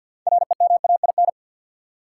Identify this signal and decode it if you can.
Morse code, heard as a keyed tone, about 52 wpm, 700 Hz. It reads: REMAIN
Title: Sentences from Top 1200 Words - New Word Drill - 50wpm